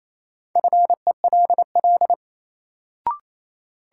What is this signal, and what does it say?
Morse code, heard as a keyed tone, about 28 wpm, 700 Hz. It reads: FELL
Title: Sets of 5 Words Encore - New Word Drill - 28wpm